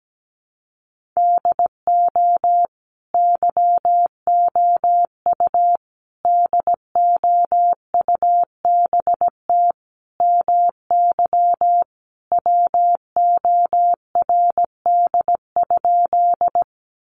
Morse code, heard as a keyed tone, about 17 wpm, 700 Hz. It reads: DO YOU DOUBT MY WORD?
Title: Sentences from Top 1500 Words 17wpm